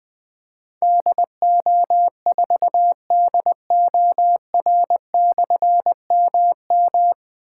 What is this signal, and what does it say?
Morse code, heard as a keyed tone, about 20 wpm, 700 Hz. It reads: DO4DOR/MM
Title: Difficult Call Signs 20wpm